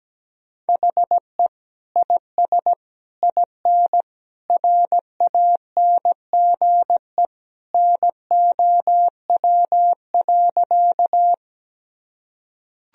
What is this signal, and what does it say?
Morse code, heard as a keyed tone, about 17 wpm, 700 Hz. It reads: HE IS IN RANGE NOW.
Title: Sentences from Top 1000 Words - Review Part 4 - 17wpm